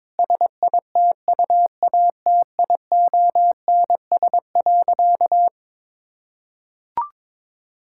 Morse code, heard as a keyed tone, about 22 wpm, 700 Hz. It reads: SITUATIONS.
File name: Difficult Sentences with Hints - 22wpm